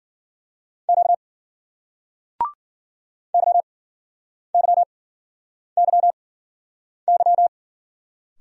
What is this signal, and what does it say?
Morse code, heard as a keyed tone, about 55 wpm, spaced as for 9 wpm, 700 Hz. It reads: Q YYYY